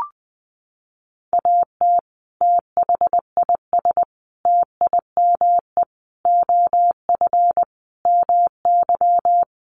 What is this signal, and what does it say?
Morse code, heard as a keyed tone, about 20 wpm, 700 Hz. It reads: AT THIS TIME OF MY
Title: Sentences from Top 300 Words 20wpm